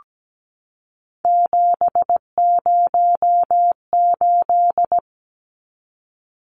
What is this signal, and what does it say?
Morse code, heard as a keyed tone, about 17 wpm, 700 Hz. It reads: 708
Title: Numbers - 3-Digit 17wpm